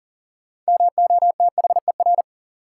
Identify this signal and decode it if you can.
Morse code, heard as a keyed tone, about 40 wpm, 700 Hz. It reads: MOTHER
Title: Sentences from Top 200 Words - New Word Drill - 40wpm